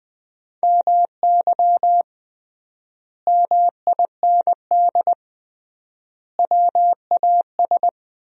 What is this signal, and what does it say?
Morse code, heard as a keyed tone, about 20 wpm, 700 Hz. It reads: MY MIND WAS
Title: 3x Word-Spacing - Sentences from Top 500 Words - 20wpm